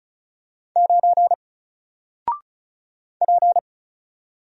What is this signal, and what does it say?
Morse code, heard as a keyed tone, about 34 wpm, 700 Hz. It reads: 9 P
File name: Single-Letter-Number-35wpm